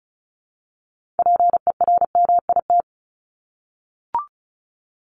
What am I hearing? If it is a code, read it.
Morse code, heard as a keyed tone, about 34 wpm, 700 Hz. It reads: PERMIT
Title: Top 1751-2000 Words 35wpm